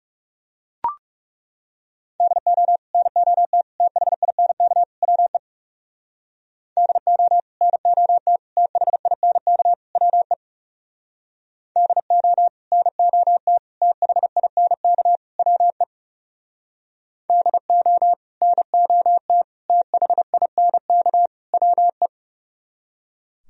Morse code, heard as a keyed tone, about 40 wpm, 690 Hz. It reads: DO NOT THINK WE DO NOT THINK WE DO NOT THINK WE DO NOT THINK WE